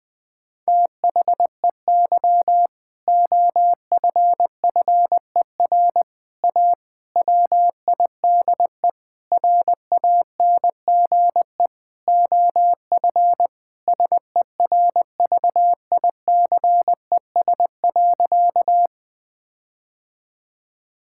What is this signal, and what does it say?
Morse code, heard as a keyed tone, about 19 wpm, 700 Hz. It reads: THEY OFFER A WIDE RANGE OF SERVICES.
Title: Sentences with 4-Word Phrases - Spoken Beginning - 20wpm